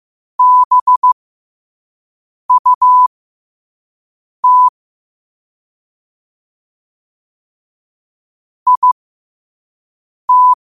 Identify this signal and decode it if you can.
Morse code, heard as a keyed tone, about 15 wpm, spaced as for 5 wpm, 1000 Hz. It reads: BUT IT
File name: qotdfiles / 5wpm